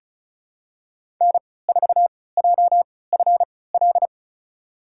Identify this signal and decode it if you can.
Morse code, heard as a keyed tone, about 34 wpm, spaced as for 20 wpm, 700 Hz. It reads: N4JFL
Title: USA Call Signs - 5 Characters - 3x Character Spacing - 35wpm